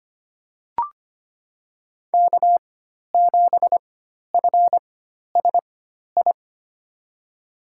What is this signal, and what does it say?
Morse code, heard as a keyed tone, about 25 wpm, spaced as for 12 wpm, 700 Hz. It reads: K7FSI
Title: USA Call Signs - 5 Characters - 4x Character Spacing - 25wpm